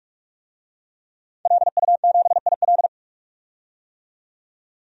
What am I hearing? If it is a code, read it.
Morse code, heard as a keyed tone, about 46 wpm, 700 Hz. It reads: LU7IL
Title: DXCC-100 Call Signs - 5 Characters - 45wpm